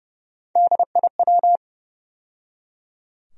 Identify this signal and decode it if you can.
Morse code, heard as a keyed tone, about 30 wpm, 700 Hz. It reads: DIW
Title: USA Call Sign Postfixes - 3 Letters 30wpm